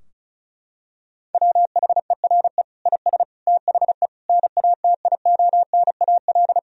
Morse code, heard as a keyed tone, about 34 wpm, 700 Hz. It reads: WHERE IS THE NATIONAL
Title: Sentences from Top 1000 Words - Review Part 4 - 35wpm